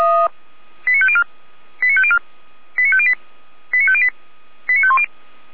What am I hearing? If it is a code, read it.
Five-tone selective call (ZVEI), heard as quick runs of five tones, tones of about 70 ms. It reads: 89504 89504 89508 89508 89510